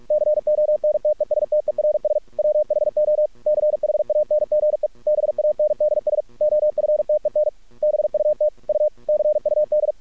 Morse code, heard as a keyed tone, about 46 wpm, 600 Hz. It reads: CONTESTERS WHO CHANGE BANDS OFTEN BUT U CUD
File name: archived / CRSnet